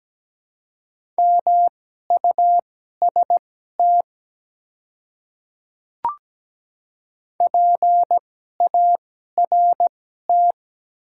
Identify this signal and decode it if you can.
Morse code, heard as a keyed tone, about 17 wpm, spaced as for 12 wpm, 700 Hz. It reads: MUST PART
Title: 2x Character Spacing - Top 200 Words - 17wpm